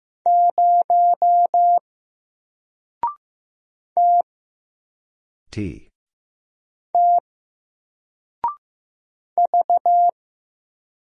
Morse code, heard as a keyed tone, about 15 wpm, 700 Hz. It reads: 0 T T V
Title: Single Character (Letter:Number:Punctuation:Prosign) 15wpm